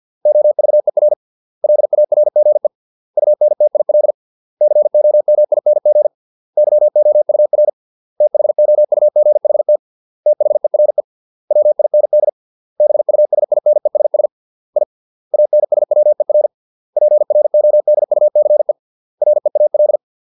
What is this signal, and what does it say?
Morse code, heard as a keyed tone, about 50 wpm, 600 Hz. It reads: OVER LARGE UNTIL COMING YOUR THOUGHT THERE WIND BUSINESS I ANSWER PRODUCE REAL